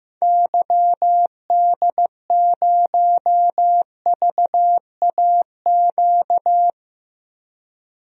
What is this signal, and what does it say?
Morse code, heard as a keyed tone, about 15 wpm, 700 Hz. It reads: YD0VAQ
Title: DXCC-100 Call Signs - 6 Characters - 15wpm